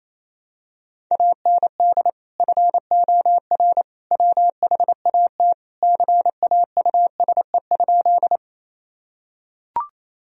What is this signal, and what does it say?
Morse code, heard as a keyed tone, about 28 wpm, 700 Hz. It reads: AND FOR WHAT CAUSE?
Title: Sentences from Top 1000 Words - Review Part 1 - 28wpm